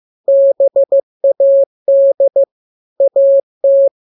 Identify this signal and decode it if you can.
Morse code, heard as a keyed tone, about 15 wpm, 550 Hz. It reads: BAD AT
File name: jokes / joke 1 15 0000